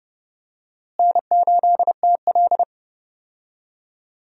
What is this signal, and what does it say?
Morse code, heard as a keyed tone, about 30 wpm, 700 Hz. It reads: N8TL
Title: USA Call Signs - 4 Characters - Repeated 3x - 30wpm